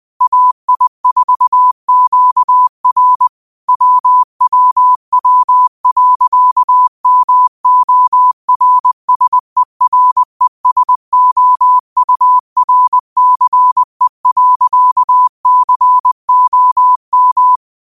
Morse code, heard as a keyed tone, about 20 wpm, 1000 Hz. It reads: AI4QR WWW.MORSERESOURCE.COM
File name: qotdfiles / 20wpm